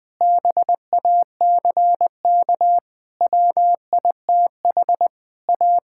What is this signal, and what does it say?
Morse code, heard as a keyed tone, about 21 wpm, 700 Hz. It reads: BACK WITH A
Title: Sentences from Top 1000 Words - Review Part 5 - 20wpm